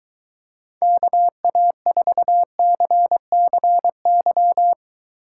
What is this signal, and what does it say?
Morse code, heard as a keyed tone, about 23 wpm, 700 Hz. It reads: KA4CCY